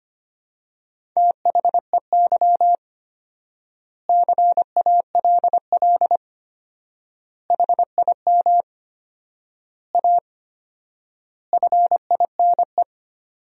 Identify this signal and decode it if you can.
Morse code, heard as a keyed tone, about 25 wpm, 700 Hz. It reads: THEY CALL HIM A FINE
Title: 4x Word-Spacing - Sentences from Top 500 Words - 25wpm